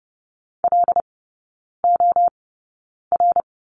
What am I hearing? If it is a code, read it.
Morse code, heard as a keyed tone, about 30 wpm, spaced as for 9 wpm, 700 Hz. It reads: LOR
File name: course / 30:10wpm / 087 Lesson 15e - Number 5 - Three Characters 30wpm